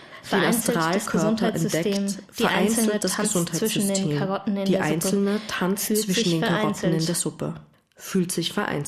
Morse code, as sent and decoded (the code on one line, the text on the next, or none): none